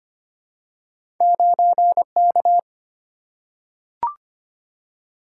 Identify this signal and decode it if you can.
Morse code, heard as a keyed tone, about 25 wpm, 690 Hz. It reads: 9K
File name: Two Letter-Number Combinations - 25wpm